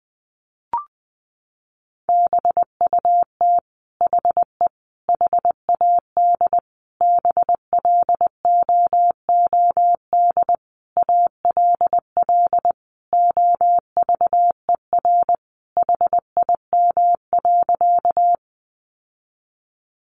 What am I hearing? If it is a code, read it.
Morse code, heard as a keyed tone, about 20 wpm, 700 Hz. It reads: BUT HE HAD BLOOD ALL OVER HIM.
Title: Sentences from Top 1100 Words 20wpm